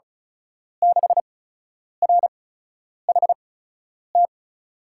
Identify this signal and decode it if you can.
Morse code, heard as a keyed tone, about 36 wpm, spaced as for 10 wpm, 700 Hz. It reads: 6RHT
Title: USA Call Signs - 5 Characters - 8x Character Spacing - 35wpm